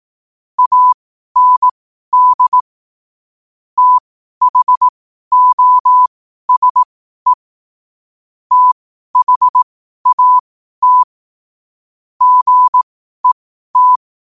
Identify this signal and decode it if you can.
Morse code, heard as a keyed tone, about 18 wpm, spaced as for 12 wpm, 1000 Hz. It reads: AND THOSE THAT GET